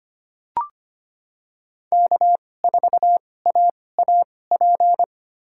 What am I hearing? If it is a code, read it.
Morse code, heard as a keyed tone, about 25 wpm, spaced as for 18 wpm, 700 Hz. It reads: K4AAP
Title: USA Call Signs - 5 Characters - 2x Character Spacing - 25wpm